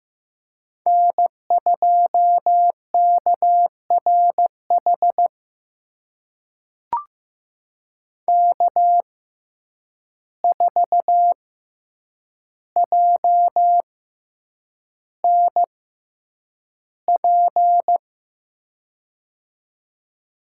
Morse code, heard as a keyed tone, about 15 wpm, spaced as for 8 wpm, 700 Hz. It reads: N2KRH K4JNP
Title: USA Call Signs - 5 Characters - 6x Character Spacing - 15wpm